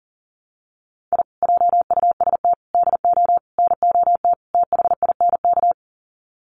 Morse code, heard as a keyed tone, about 40 wpm, 700 Hz. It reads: I JUST DO NOT THINK
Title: Sets of 5 Words Encore 40wpm